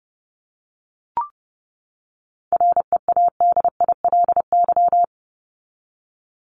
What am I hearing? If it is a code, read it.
Morse code, heard as a keyed tone, about 30 wpm, 700 Hz. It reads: READILY
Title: LY Suffix - Top 100 Words with Base Words 30wpm